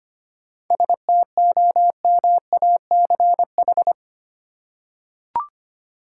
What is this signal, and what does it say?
Morse code, heard as a keyed tone, about 25 wpm, 700 Hz. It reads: STOMACH
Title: Seven Letter Words - 25wpm